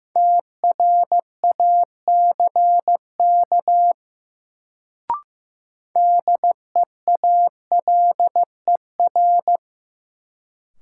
Morse code, heard as a keyed tone, about 15 wpm, 700 Hz. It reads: TRACK DEALER
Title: Sentences from Top 1100 Words - New Word Drill - 15wpm